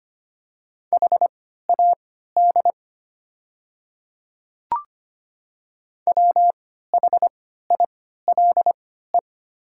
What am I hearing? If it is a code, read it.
Morse code, heard as a keyed tone, about 25 wpm, spaced as for 14 wpm, 700 Hz. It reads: HAD WHILE
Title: 3x Character Spacing - Top 300 Words - 25wpm